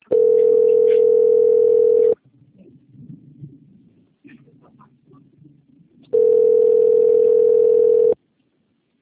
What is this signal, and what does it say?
Ringback tone (North America), 440 + 480 Hz.
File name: out- - - - - .19051